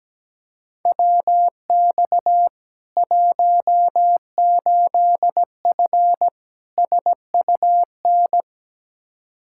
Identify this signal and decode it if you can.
Morse code, heard as a keyed tone, about 17 wpm, 700 Hz. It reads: WX 18F SUN
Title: Practice QSO - Exchange 2 - Info - 17wpm